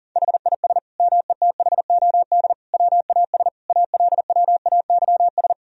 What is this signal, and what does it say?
Morse code, heard as a keyed tone, about 37 wpm, 700 Hz. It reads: HIS METHOD WAS ALWAYS
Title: Sentences from Top 1000 Words - Review Part 3 - 40wpm